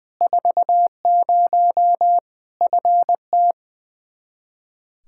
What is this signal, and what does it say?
Morse code, heard as a keyed tone, about 21 wpm, 700 Hz. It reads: 40 FT
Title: Practice QSO - Exchange 2 - Info - 20wpm